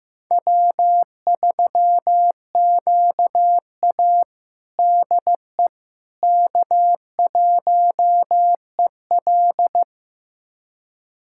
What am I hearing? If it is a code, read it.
Morse code, heard as a keyed tone, about 15 wpm, 700 Hz. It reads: W3QA DE K1EL
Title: USA Callsign DE Callsign - 4 Characters - 15wpm